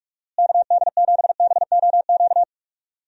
Morse code, heard as a keyed tone, about 46 wpm, 700 Hz. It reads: KD7BOQ
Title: USA Call Signs - 6 Characters - Repeated 3x - 45wpm